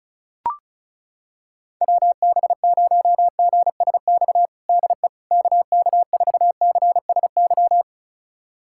Morse code, heard as a keyed tone, about 34 wpm, 700 Hz. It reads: WB0GSX DE KK4CSY
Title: USA Callsign DE Callsign - 6 Characters - 35wpm